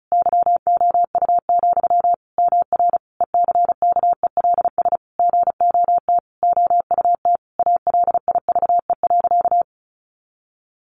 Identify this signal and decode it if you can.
Morse code, heard as a keyed tone, about 34 wpm, 700 Hz. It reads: YOU, MR ECKELS GOT OUT ALIVE.
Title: A Sound Of Thunder 35wpm